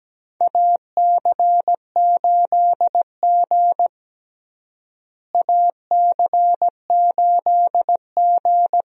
Morse code, heard as a keyed tone, about 17 wpm, 700 Hz. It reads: AC8G AC8G
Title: USA Call Signs - 4 Characters - Repeated 2x - 17wpm